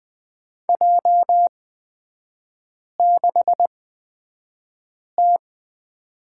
Morse code, heard as a keyed tone, about 20 wpm, spaced as for 5 wpm, 700 Hz. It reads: J6T
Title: 224 Lesson 36e - BT - Three Characters 20wpm